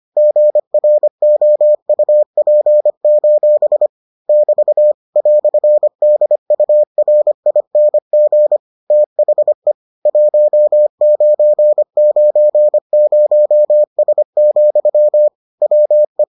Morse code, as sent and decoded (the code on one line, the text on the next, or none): --. .-. --- ..- .--. ---... / -...- / .-..-. -.. ..- .-. .. -. --. / - .... / .---- ----. ----. ----- ... --..-- / .-- .
GROUP: <BT> "DURING THE 1990S, WE